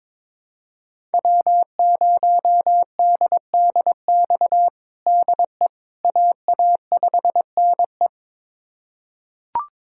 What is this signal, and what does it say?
Morse code, heard as a keyed tone, about 22 wpm, 700 Hz. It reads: W0DDX DE AA5NE
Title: USA Callsign DE Callsign - 5 Characters - 22wpm